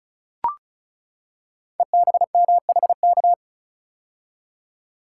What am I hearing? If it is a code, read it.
Morse code, heard as a keyed tone, about 34 wpm, 700 Hz. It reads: EBMHK
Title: Code Groups - 5 Letters - 35wpm